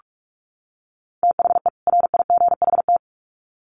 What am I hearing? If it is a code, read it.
Morse code, heard as a keyed tone, about 44 wpm, 700 Hz. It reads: THE RIGHT
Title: Sets of 2 Words 45wpm